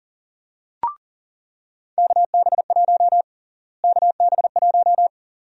Morse code, heard as a keyed tone, about 37 wpm, 700 Hz. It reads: KB1 KB1
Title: USA Call Sign Prefixes - Repeated 2x - 40wpm